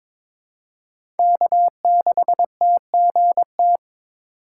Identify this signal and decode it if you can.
Morse code, heard as a keyed tone, about 22 wpm, 700 Hz. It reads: K6TGT
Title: USA Call Signs - 2x Character Spacing - 22wpm